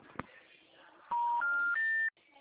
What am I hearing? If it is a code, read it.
Special information tone.